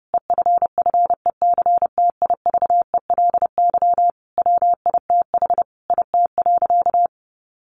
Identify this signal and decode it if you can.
Morse code, heard as a keyed tone, about 30 wpm, 700 Hz. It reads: EFFECTIVELY WITH IT.
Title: Sentences from Top 1400 Words 30wpm